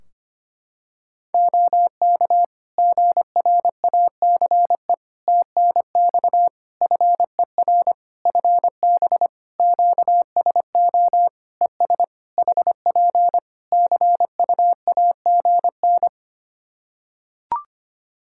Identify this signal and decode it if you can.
Morse code, heard as a keyed tone, about 25 wpm, 700 Hz. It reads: OK GRACE TNX FER FB QSO ES HP CUAGN
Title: Practice QSO - Exchange 4 - Ending - 25wpm